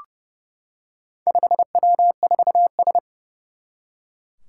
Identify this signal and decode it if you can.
Morse code, heard as a keyed tone, about 30 wpm, 700 Hz. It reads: 5W4S
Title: Four Challenging Letter-Number Combinations 30wpm